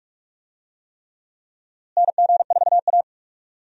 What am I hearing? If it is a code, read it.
Morse code, heard as a keyed tone, about 46 wpm, 700 Hz. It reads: NG4A
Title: USA Call Signs - 4 Characters - Repeated 3x - 45wpm